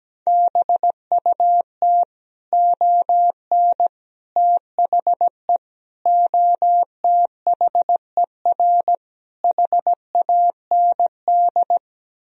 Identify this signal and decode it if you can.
Morse code, heard as a keyed tone, about 17 wpm, 700 Hz. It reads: BUT ON THE OTHER HAND
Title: Speed-Racing - Sets of 5 Words 17wpm